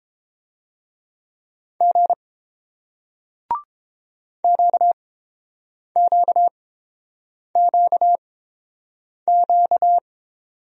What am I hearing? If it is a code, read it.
Morse code, heard as a keyed tone, about 31 wpm, spaced as for 7 wpm, 700 Hz. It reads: G QQQQ